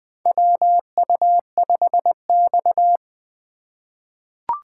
Morse code, heard as a keyed tone, about 21 wpm, 700 Hz. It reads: WU5X
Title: USA Call Signs - 2x Character Spacing - 20wpm